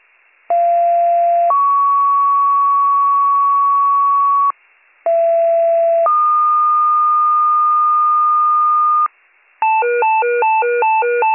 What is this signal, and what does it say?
Two-tone page, heard as a one-second tone, then a three-second tone, about 690 then 1120 Hz, about 670 then 1220 Hz. It is heard from 0.5 to 9.1 s.